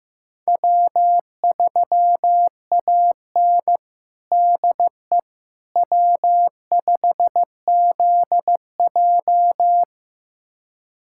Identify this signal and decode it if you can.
Morse code, heard as a keyed tone, about 15 wpm, 700 Hz. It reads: W3AN DE W5ZJ
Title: USA Callsign DE Callsign - 4 Characters - 15wpm